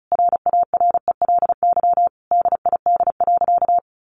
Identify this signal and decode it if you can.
Morse code, heard as a keyed tone, about 34 wpm, 700 Hz. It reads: RARELY DID.
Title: Sentences from Top 1400 Words 35wpm